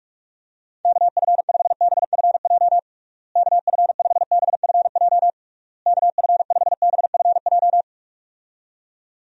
Morse code, heard as a keyed tone, about 41 wpm, 700 Hz. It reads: KF5BFJ KF5BFJ KF5BFJ
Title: USA Call Signs - 6 Characters - Repeated 3x - 45wpm